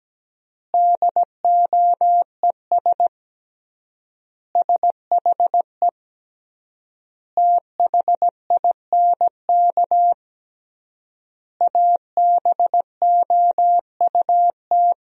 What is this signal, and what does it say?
Morse code, heard as a keyed tone, about 17 wpm, 700 Hz. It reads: DOES SHE THINK ABOUT
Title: 3x Word-Spacing - Sentences from Top 200 Words - 17wpm